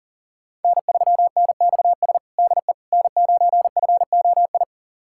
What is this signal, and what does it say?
Morse code, heard as a keyed tone, about 37 wpm, 700 Hz. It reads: N3NXS DE N9FOI
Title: USA Callsign DE Callsign - 5 Characters - 40wpm